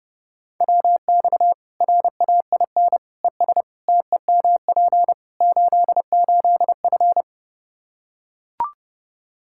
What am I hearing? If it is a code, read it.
Morse code, heard as a keyed tone, about 30 wpm, 700 Hz. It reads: WX RAIN ES TEMP 88F
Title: Practice QSO - Exchange 2 - Info - 30wpm